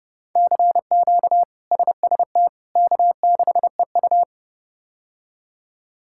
Morse code, heard as a keyed tone, about 30 wpm, 700 Hz. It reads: CQ SST K6EU
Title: K1USN SST QSO Practice - 30wpm